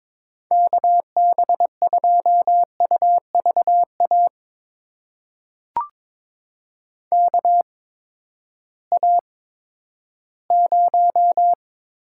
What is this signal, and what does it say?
Morse code, heard as a keyed tone, about 22 wpm, spaced as for 12 wpm, 700 Hz. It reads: KB2UVA KA0
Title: USA Call Signs - 6 Characters - 8x Character Spacing - 22wpm